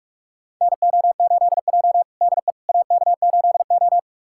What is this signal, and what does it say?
Morse code, heard as a keyed tone, about 42 wpm, 700 Hz. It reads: NO8J DE AK8O
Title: USA Callsign DE Callsign - 4 Characters - 45wpm